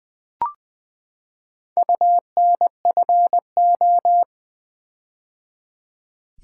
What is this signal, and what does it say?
Morse code, heard as a keyed tone, about 20 wpm, 700 Hz. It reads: UNFO
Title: Build A Word - 4 Letters At A Time - 11-Letter Words - 20wpm